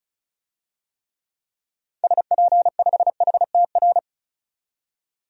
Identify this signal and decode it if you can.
Morse code, heard as a keyed tone, about 34 wpm, 700 Hz. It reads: SP5HTR
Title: DXCC-100 Call Signs - 6 Characters - 35wpm